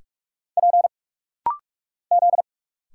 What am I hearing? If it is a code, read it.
Morse code, heard as a keyed tone, about 46 wpm, 700 Hz. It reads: P Z
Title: Single Letters - Rapid-Fire 45wpm